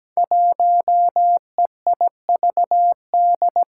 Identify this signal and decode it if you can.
Morse code, heard as a keyed tone, about 17 wpm, 700 Hz. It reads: 1EIVD
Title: Five Challenging Letter-Number Combinations 17wpm